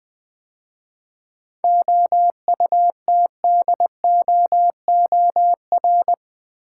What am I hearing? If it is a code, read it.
Morse code, heard as a keyed tone, about 20 wpm, 700 Hz. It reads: OUTDOOR
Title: 7 Letter Words - First 3 Letters Given - 20wpm